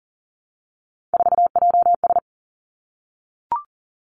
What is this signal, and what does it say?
Morse code, heard as a keyed tone, about 41 wpm, 700 Hz. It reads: 4JS
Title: Three Challenging Letter-Number Combinations 40wpm